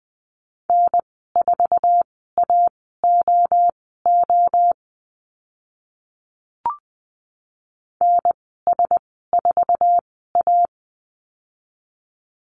Morse code, heard as a keyed tone, about 20 wpm, spaced as for 14 wpm, 700 Hz. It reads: N4AOO NS4A
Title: 055 Lesson 10g - Letter r - Call Signs 20wpm